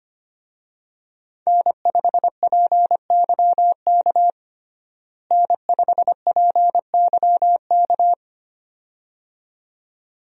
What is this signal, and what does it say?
Morse code, heard as a keyed tone, about 25 wpm, 700 Hz. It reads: N5PYK N5PYK
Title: USA Call Signs - 5 Characters - Repeated 2x - 25wpm